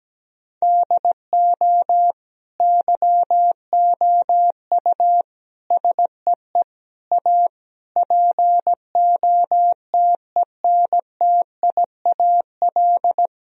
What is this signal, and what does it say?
Morse code, heard as a keyed tone, about 17 wpm, 700 Hz. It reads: DO YOU SEE A POTENTIAL
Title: Sentences from Top 1000 Words - Review Part 4 - 17wpm